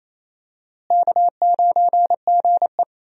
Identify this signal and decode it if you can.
Morse code, heard as a keyed tone, about 28 wpm, 700 Hz. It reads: K9GE
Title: USA Call Signs - 4 Characters - Repeated 3x - 28wpm